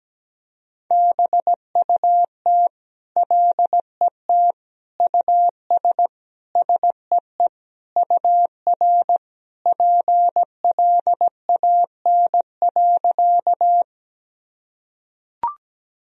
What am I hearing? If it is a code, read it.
Morse code, heard as a keyed tone, about 17 wpm, 700 Hz. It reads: BUT LET US SEE UR PLAN.
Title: Sentences from Top 500 Words and Abbreviations 17wpm